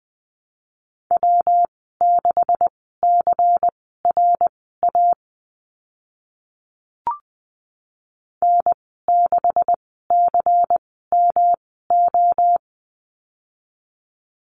Morse code, heard as a keyed tone, about 20 wpm, spaced as for 14 wpm, 700 Hz. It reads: W6CRA N6CMO
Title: 167 Lesson 27g - Number 7 - Call Signs 20wpm